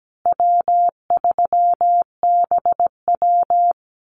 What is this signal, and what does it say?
Morse code, heard as a keyed tone, about 17 wpm, 700 Hz. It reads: W3BW
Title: Easy Call Signs Repeated 3x - 17wpm